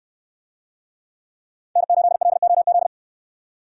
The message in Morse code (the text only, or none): N8LOZ